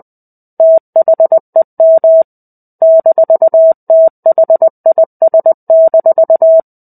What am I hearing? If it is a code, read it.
Morse code, heard as a keyed tone, about 20 wpm, 650 Hz. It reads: THEM -THIS-
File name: 05-Chapter-05 20 WPM